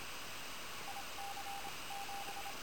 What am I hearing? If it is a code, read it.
Morse code, heard as a keyed tone, about 26 wpm, 800 Hz, with radio noise. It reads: IC 7
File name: attachments / QSO HA9PP